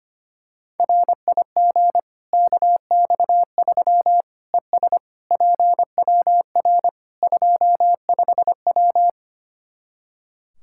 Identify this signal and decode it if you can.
Morse code, heard as a keyed tone, about 25 wpm, 700 Hz. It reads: RIG KX3 ES PWR 25W
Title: K1USN SST QSO Practice 25wpm